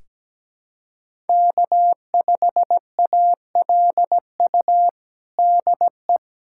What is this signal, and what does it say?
Morse code, heard as a keyed tone, about 17 wpm, 700 Hz. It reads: K5ALU DE
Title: USA Callsign DE Callsign - 5 Characters - 17wpm